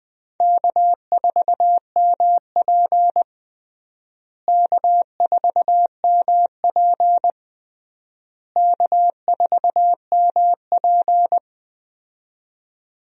Morse code, heard as a keyed tone, about 20 wpm, 700 Hz. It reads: K4MP K4MP K4MP
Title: USA Call Signs - 4 Characters - Repeated 3x - 20wpm